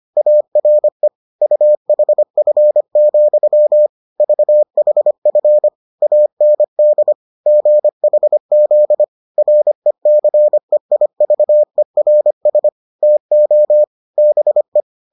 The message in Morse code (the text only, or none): ARE UHF, VHF AND GHZ RECEIVERS TO BE